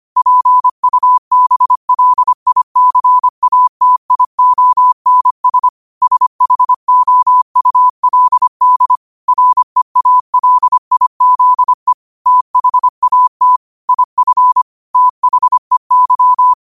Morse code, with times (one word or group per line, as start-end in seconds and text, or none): 0.2-5.7 PUBLICATIONS
6.0-9.0 SHOULD
9.3-11.9 REALIZE
12.3-13.6 THAT
13.9-14.6 IF
14.9-16.5 THEY